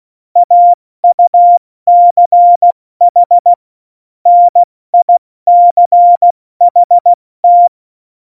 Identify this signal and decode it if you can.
Morse code, heard as a keyed tone, about 16 wpm, spaced as for 14 wpm, 700 Hz. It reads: AUCH NICHT
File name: Morsekurs / Witze / Witz 11. September 2018, 70 F80